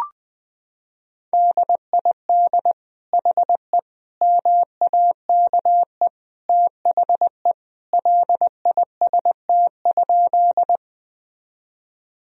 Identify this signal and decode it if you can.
Morse code, heard as a keyed tone, about 20 wpm, 700 Hz. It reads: DID HE MAKE THE LIST?
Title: Sentences from Top 1000 Words - Review Part 2 - 20wpm